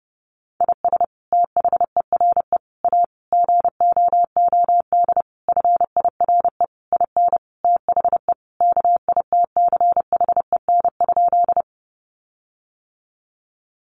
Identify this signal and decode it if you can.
Morse code, heard as a keyed tone, about 30 wpm, 700 Hz. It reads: IS THERE A GOOD FIRE IN THE KITCHEN?
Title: Sentences from Top 1400 Words 30wpm